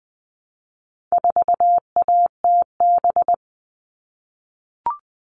Morse code, heard as a keyed tone, about 20 wpm, 700 Hz. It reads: 4ATB
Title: All Cut Numbers - 4-Digit 20wpm